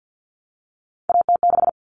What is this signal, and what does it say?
Morse code, heard as a keyed tone, about 50 wpm, 700 Hz. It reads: AT6
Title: All Cut Numbers - 3-Digit 50wpm